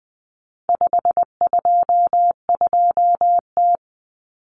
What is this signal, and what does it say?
Morse code, heard as a keyed tone, about 20 wpm, 700 Hz. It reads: 522T
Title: Common Cut Numbers - 4-Digit 20wpm